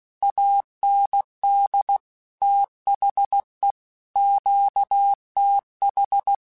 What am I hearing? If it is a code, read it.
Morse code, heard as a keyed tone, about 16 wpm, 800 Hz. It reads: AND THE QTH